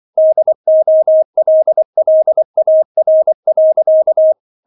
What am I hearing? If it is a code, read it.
Morse code, heard as a keyed tone, about 24 wpm, 630 Hz. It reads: DOLLAR.